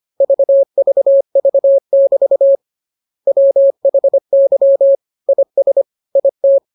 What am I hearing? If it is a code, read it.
Morse code, heard as a keyed tone, about 25 wpm, 550 Hz. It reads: VVV<BT> WHY IS IT
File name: jokes / joke 9 25 0000